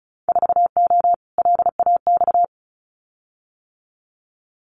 Morse code, heard as a keyed tone, about 36 wpm, 700 Hz. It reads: 4O LAX
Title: Winter Field Day Exchanges Encore 35wpm